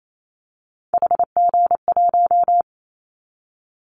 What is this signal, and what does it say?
Morse code, heard as a keyed tone, about 28 wpm, 700 Hz. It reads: HG1
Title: Callsign Prefixes for Top 50 DXCC - 28wpm